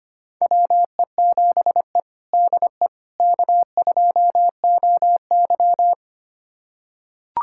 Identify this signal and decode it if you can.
Morse code, heard as a keyed tone, about 25 wpm, 700 Hz. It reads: WE7E DE K2OY
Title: USA Callsign DE Callsign - 4 Characters - 25wpm